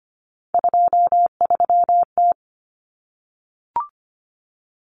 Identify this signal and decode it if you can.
Morse code, heard as a keyed tone, about 25 wpm, 700 Hz. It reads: 23T
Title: Common Cut Numbers - 3-Digit 25wpm